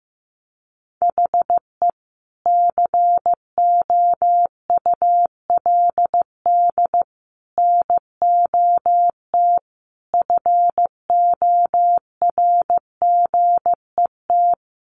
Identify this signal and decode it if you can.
Morse code, heard as a keyed tone, about 15 wpm, 700 Hz. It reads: HE COULD NOT FORGET
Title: Sentences from Top 1200 Words 15wpm